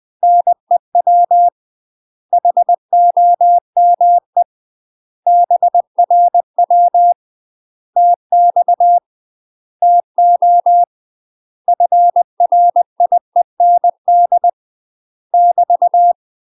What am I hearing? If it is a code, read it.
Morse code, heard as a keyed tone, about 20 wpm, 700 Hz. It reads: NEW HOME BRW TX TO FRIEND <BT>
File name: UK MASTER QSOs TRAINING / 20WPM / UK MASTER QSOs 0025